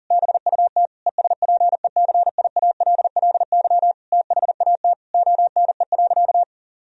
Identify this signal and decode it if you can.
Morse code, heard as a keyed tone, about 37 wpm, 700 Hz. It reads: BUT ESPECIALLY THAT ONE.
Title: Sentences from Top 1000 Words - Review Part 4 - 40wpm